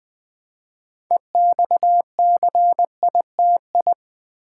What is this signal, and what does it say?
Morse code, heard as a keyed tone, about 20 wpm, 700 Hz. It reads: EXCITI
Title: Build A Word - 3 Letters At A Time - 8-Letter Words - 20wpm